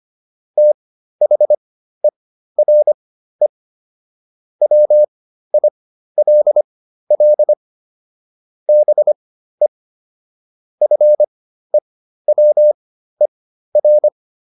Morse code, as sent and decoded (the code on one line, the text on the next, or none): - .... .-. / .-- .. .-.. .-.. / -... / ..-. .-- . .-.
THERE WILL BE FEWER